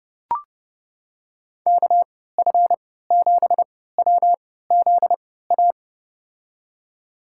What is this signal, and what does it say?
Morse code, heard as a keyed tone, about 30 wpm, spaced as for 17 wpm, 700 Hz. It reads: KF7WZA
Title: USA Call Signs - 6 Characters - 3x Character Spacing - 30wpm